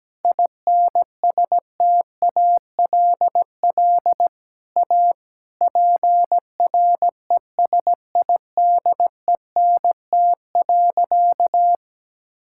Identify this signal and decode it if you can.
Morse code, heard as a keyed tone, about 17 wpm, 700 Hz. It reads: INSTALL A PRESIDENT.
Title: Sentences from Top 1600 Words 17wpm